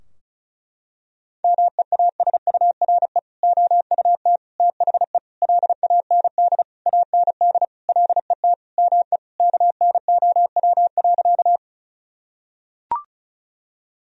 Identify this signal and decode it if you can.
Morse code, heard as a keyed tone, about 34 wpm, 700 Hz. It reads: MEASURE OUT THE LAND AND LET ME KNOW.
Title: Sentences from Top 1000 Words - Review Part 2 - 35wpm